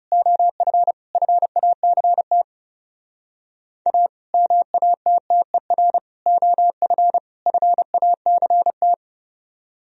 Morse code, heard as a keyed tone, about 37 wpm, 700 Hz. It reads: OF FACT A MATTER OF FACT
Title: Speed-Racing - Sets of 4 Words 30wpm